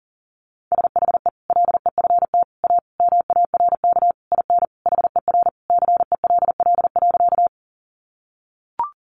Morse code, heard as a keyed tone, about 40 wpm, 700 Hz. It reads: SHE LEFT A MARK IN HER CELL.
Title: Sentences from Top 1100 Words 40wpm